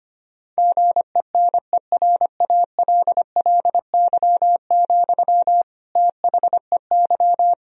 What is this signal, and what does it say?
Morse code, heard as a keyed tone, about 25 wpm, 700 Hz. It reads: GENERALLY, THEY
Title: Sentences from Top 1000 Words - Review Part 4 - 25wpm